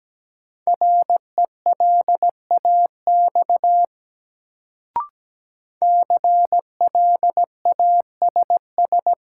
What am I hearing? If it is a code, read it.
Morse code, heard as a keyed tone, about 17 wpm, 700 Hz. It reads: RELAX CLASS
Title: Sentences from Top 1400 Words - New Word Drill Alt - 17wpm